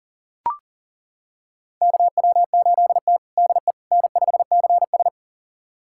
Morse code, heard as a keyed tone, about 40 wpm, 700 Hz. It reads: KW8T DE N5CS
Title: USA Callsign DE Callsign - 4 Characters - 40wpm